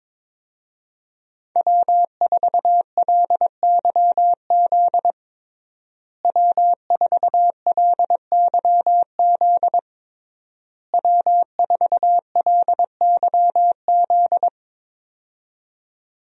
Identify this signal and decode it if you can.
Morse code, heard as a keyed tone, about 22 wpm, 700 Hz. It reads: W4LYZ W4LYZ W4LYZ
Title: USA Call Signs - 5 Characters - Repeated 3x - 22wpm